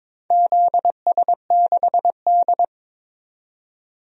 Morse code, heard as a keyed tone, about 22 wpm, 700 Hz. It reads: ZS6D
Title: DXCC-100 Call Signs - 4 Characters - 22wpm